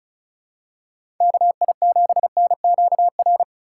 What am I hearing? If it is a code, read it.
Morse code, heard as a keyed tone, about 36 wpm, 700 Hz. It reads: KI7NQR
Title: USA Call Signs - 6 Characters - 3x Character Spacing - 35wpm